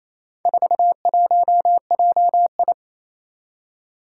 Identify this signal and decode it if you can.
Morse code, heard as a keyed tone, about 27 wpm, 700 Hz. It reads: 41JI
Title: Four Challenging Letter-Number Combinations 28wpm